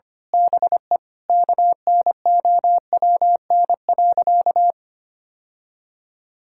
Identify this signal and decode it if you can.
Morse code, heard as a keyed tone, about 25 wpm, 700 Hz. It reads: BE KNOWN.
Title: Sentences from Top 1000 Words - Review Part 6 - 25wpm